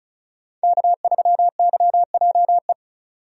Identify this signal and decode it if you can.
Morse code, heard as a keyed tone, about 36 wpm, 700 Hz. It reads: K3YJE
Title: USA Call Signs - 5 Characters - 35wpm